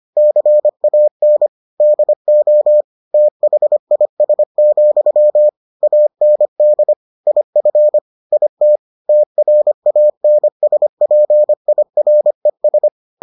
Morse code, heard as a keyed tone, about 25 wpm, 600 Hz. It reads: CAN DO THIS, AND IF IT TRANSPIRES